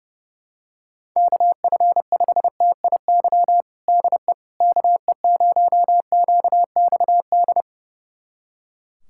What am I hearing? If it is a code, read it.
Morse code, heard as a keyed tone, about 32 wpm, 700 Hz. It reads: KF5TIY DE KE0QXD
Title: USA Callsign DE Callsign - 6 Characters - 30wpm